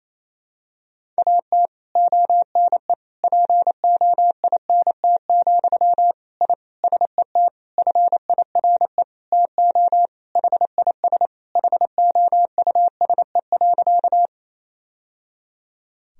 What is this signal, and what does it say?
Morse code, heard as a keyed tone, about 28 wpm, 700 Hz. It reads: AT ONE POINT, I SET FIRE TO HIS HOUSE.
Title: Sentences from Top 1000 Words - Review Part 1 - 28wpm